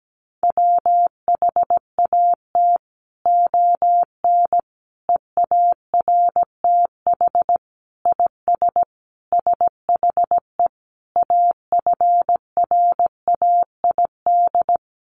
Morse code, heard as a keyed tone, about 17 wpm, 700 Hz. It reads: WHAT ON EARTH IS SHE AFRAID
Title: Sentences from Top 1500 Words 17wpm